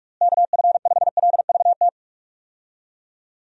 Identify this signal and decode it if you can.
Morse code, heard as a keyed tone, about 46 wpm, 700 Hz. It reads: KF5LVT
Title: USA Call Signs - 6 Characters - Repeated 3x - 45wpm